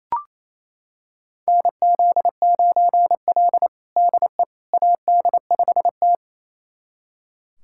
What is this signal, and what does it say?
Morse code, heard as a keyed tone, about 27 wpm, 700 Hz. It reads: NZ9L DE AD5T
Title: USA Callsign DE Callsign - 4 Characters - 28wpm